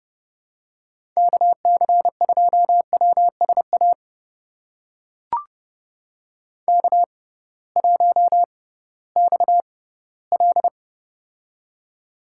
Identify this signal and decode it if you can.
Morse code, heard as a keyed tone, about 31 wpm, spaced as for 18 wpm, 700 Hz. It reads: KC2WSA K1XL
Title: USA Call Signs - 6x Character Spacing - 30wpm